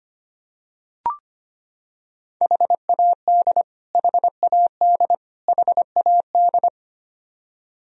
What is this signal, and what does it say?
Morse code, heard as a keyed tone, about 25 wpm, 700 Hz. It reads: HAD HAD HAD
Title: Top 100 Words Repeated 25wpm